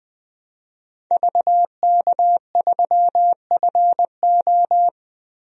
Morse code, heard as a keyed tone, about 20 wpm, 700 Hz. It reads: VK3FO
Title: DXCC-100 Call Signs - 5 Characters - 20wpm